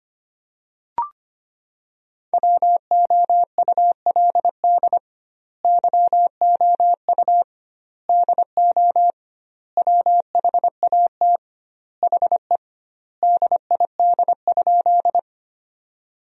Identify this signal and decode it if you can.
Morse code, heard as a keyed tone, about 25 wpm, 700 Hz. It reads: WOULD YOU DO WHAT HE DID?
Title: 2x Word-Spacing - Sentences from Top 100 Words - 25wpm